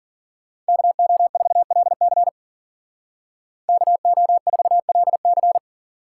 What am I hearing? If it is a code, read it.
Morse code, heard as a keyed tone, about 49 wpm, 700 Hz. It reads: KO4LC KO4LC
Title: Speed Racing - USA 2x2 Call Signs 40wpm